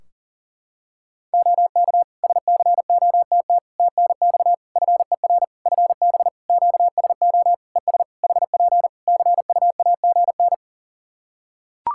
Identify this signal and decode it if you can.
Morse code, heard as a keyed tone, about 40 wpm, 700 Hz. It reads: OK SCOTT TNX FER FB QSO ES HP CUAGN
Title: Practice QSO - Exchange 4 - Ending - 40wpm